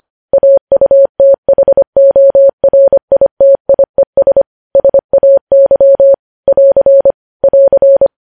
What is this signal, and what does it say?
Morse code, heard as a keyed tone, about 25 wpm, 550 Hz. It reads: AUTHORITIES SAY <AR> <AR>
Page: news-headlines-in-morse-code-at-25-wpm